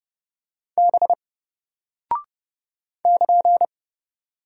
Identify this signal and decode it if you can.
Morse code, heard as a keyed tone, about 30 wpm, 700 Hz. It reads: B <KN>
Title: Single Character (Letter:Number:Punctuation:Prosign) Alt 30wpm